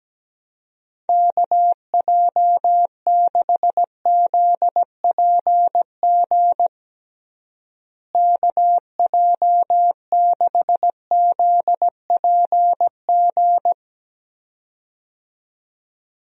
Morse code, heard as a keyed tone, about 17 wpm, 700 Hz. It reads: KJ6ZPG KJ6ZPG
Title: USA Call Signs - 6 Characters - Repeated 2x - 17wpm